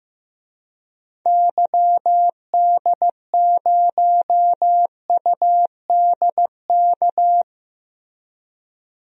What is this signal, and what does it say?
Morse code, heard as a keyed tone, about 15 wpm, 700 Hz. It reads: YD0UDK
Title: DXCC-100 Call Signs - 6 Characters - Repeated 3x - 15wpm